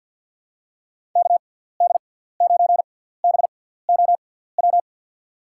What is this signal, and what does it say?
Morse code, heard as a keyed tone, about 53 wpm, spaced as for 17 wpm, 700 Hz. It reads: KD9BOW
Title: USA Call Signs - 6 Characters - 6x Character Spacing - 50wpm